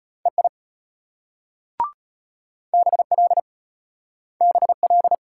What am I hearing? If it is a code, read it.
Morse code, heard as a keyed tone, about 33 wpm, 700 Hz. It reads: EI BL BL